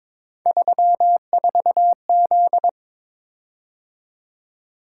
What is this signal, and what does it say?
Morse code, heard as a keyed tone, about 22 wpm, 700 Hz. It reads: 34Z
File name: practice / 22wpm / Three Letter-Number Combinations - 22wpm